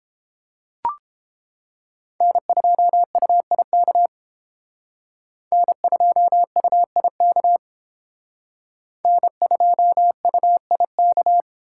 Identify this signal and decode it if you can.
Morse code, heard as a keyed tone, about 29 wpm, 700 Hz. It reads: N2UIK N2UIK N2UIK